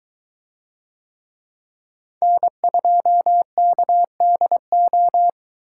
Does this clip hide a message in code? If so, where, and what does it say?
Yes: Morse code from 2.2 to 5.3 s, heard as a keyed tone, about 21 wpm, 700 Hz. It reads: N2KDO